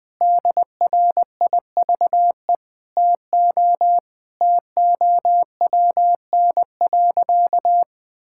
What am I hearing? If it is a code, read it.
Morse code, heard as a keyed tone, about 20 wpm, 700 Hz. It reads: DRIVE TO TOWN.
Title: Sentences from Top 1000 Words - Review Part 6 - 20wpm